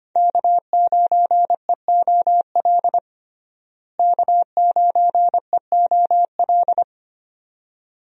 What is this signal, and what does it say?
Morse code, heard as a keyed tone, about 25 wpm, 700 Hz. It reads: K9EOL K9EOL
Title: USA Call Signs - 5 Characters - Repeated 2x - 25wpm